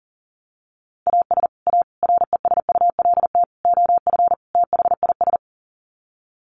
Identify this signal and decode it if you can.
Morse code, heard as a keyed tone, about 40 wpm, 700 Hz. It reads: AS A RESULT OF THIS
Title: Sets of 5 Words 40wpm